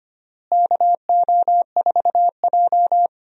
Morse code, heard as a keyed tone, about 25 wpm, 700 Hz. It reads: KO4J
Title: USA Call Signs - 4 Characters - Repeated 2x - 25wpm